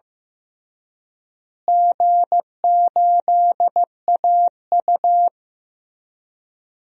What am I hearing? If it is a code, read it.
Morse code, heard as a keyed tone, about 15 wpm, 700 Hz. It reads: G8AU
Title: DXCC-100 Call Signs - 4 Characters - Repeated 2x - 15wpm